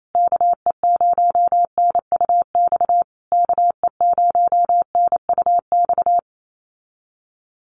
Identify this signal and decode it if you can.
Morse code, heard as a keyed tone, about 28 wpm, 700 Hz. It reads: KE0NUX KE0NUX
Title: USA Call Signs Repeated 3x 28wpm